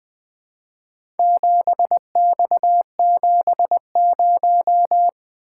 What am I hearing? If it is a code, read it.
Morse code, heard as a keyed tone, about 20 wpm, 700 Hz. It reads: 7X70
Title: Four Letter-Number Combinations - 20wpm